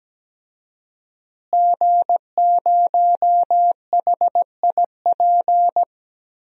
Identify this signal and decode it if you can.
Morse code, heard as a keyed tone, about 17 wpm, 700 Hz. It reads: G0HIP
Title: DXCC-100 Call Signs - 5 Characters - Repeated 3x - 17wpm